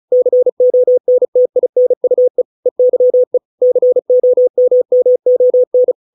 Morse code, heard as a keyed tone, about 34 wpm, 500 Hz. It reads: CONTINUE EYE COMMON